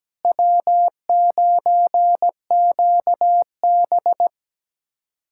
Morse code, heard as a keyed tone, about 17 wpm, 700 Hz. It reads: W9QB
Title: USA Callsign DE Callsign - 4 Characters - 17wpm